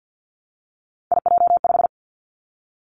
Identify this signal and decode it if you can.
Morse code, heard as a keyed tone, about 50 wpm, 700 Hz. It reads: IJ5
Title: Three Challenging Letter-Number Combinations 50wpm